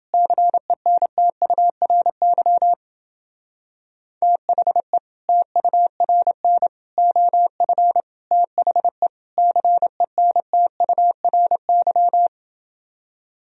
Morse code, heard as a keyed tone, about 28 wpm, 700 Hz. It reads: CENTURY THE TURN OF THE CENTURY